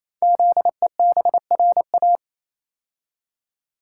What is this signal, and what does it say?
Morse code, heard as a keyed tone, about 27 wpm, 700 Hz. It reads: ZEBRA
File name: Easy Words - Speed-Racing with Hints - 22wpm